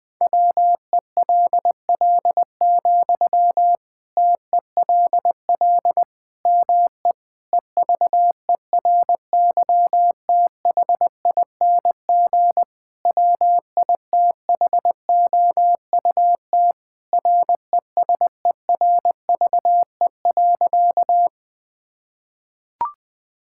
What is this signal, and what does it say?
Morse code, heard as a keyed tone, about 20 wpm, 700 Hz. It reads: WELL, TELL ME EVERYTHING WITHOUT RESERVE.
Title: Sentences from Top 1000 Words - Review Part 6 - 20wpm